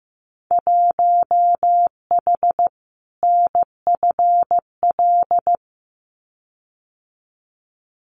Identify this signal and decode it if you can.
Morse code, heard as a keyed tone, about 15 wpm, 700 Hz. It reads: 1H NFL
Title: Winter Field Day Exchanges 15wpm